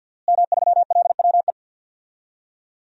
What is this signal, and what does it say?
Morse code, heard as a keyed tone, about 50 wpm, 700 Hz. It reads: M3LWE
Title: DXCC-100 Call Signs - 5 Characters - Repeated 2x - 50wpm